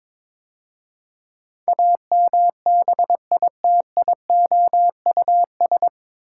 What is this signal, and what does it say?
Morse code, heard as a keyed tone, about 22 wpm, 700 Hz. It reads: AMBITIOUS
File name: 9 Letter Words - First 3 Letters Given - 22wpm